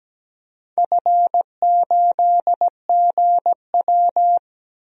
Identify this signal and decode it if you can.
Morse code, heard as a keyed tone, about 17 wpm, 700 Hz. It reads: F8GW
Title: DXCC-100 Call Signs - 4 Characters - 17wpm